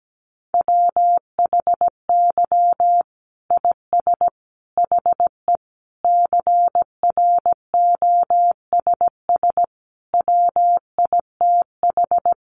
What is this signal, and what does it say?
Morse code, heard as a keyed tone, about 17 wpm, 700 Hz. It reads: WHY IS HE CROSS WITH
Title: Sentences from Top 300 Words 17wpm